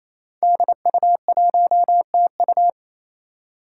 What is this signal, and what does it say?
Morse code, heard as a keyed tone, about 28 wpm, 700 Hz. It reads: DU1TU
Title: Five Challenging Letter-Number Combinations 28wpm